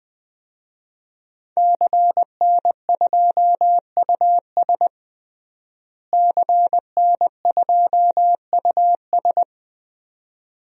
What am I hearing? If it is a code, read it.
Morse code, heard as a keyed tone, about 20 wpm, 700 Hz. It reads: CN2US CN2US
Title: DXCC-100 Call Signs - 5 Characters - Repeated 2x - 20wpm